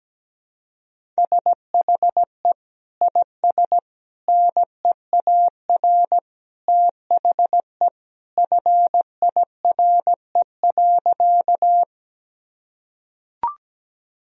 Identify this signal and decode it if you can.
Morse code, heard as a keyed tone, about 17 wpm, 700 Hz. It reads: SHE IS NEAR THE FIRE.
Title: Sentences from Top 1000 Words - Review Part 1 - 17wpm